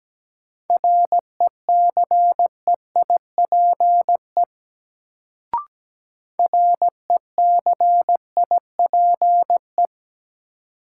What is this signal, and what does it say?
Morse code, heard as a keyed tone, about 17 wpm, 700 Hz. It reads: RECEIPE RECIPE
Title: Sentences from Top 1200 Words - New Word Drill Alt - 17wpm